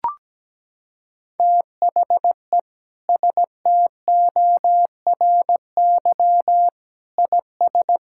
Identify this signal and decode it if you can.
Morse code, heard as a keyed tone, about 17 wpm, 700 Hz. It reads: THE STORY IS
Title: Sentences from Top 1000 Words - Review Part 5 - 17wpm